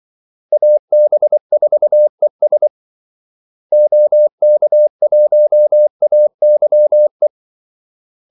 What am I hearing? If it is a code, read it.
Morse code, heard as a keyed tone, about 24 wpm, 600 Hz. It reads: AB4ES OK1AYE
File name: Callsigns-24wpm